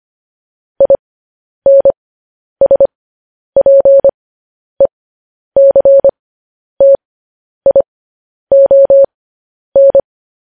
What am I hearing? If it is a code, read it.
Morse code, heard as a keyed tone, about 25 wpm, spaced as for 10 wpm, 550 Hz. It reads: INSPECTION
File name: podcast / News headlines-1x-25x10